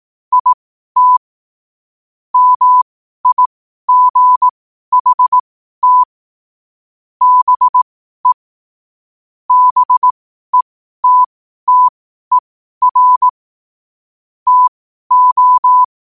Morse code, heard as a keyed tone, about 18 wpm, spaced as for 12 wpm, 1000 Hz. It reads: IT MIGHT BE BETTER TO
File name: qotdfiles / 12wpm